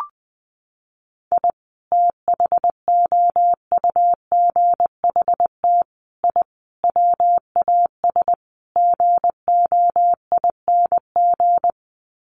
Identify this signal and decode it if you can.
Morse code, heard as a keyed tone, about 20 wpm, 700 Hz. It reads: I THOUGHT I WAS GOING
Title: Sets of 5 Words Encore2 20wpm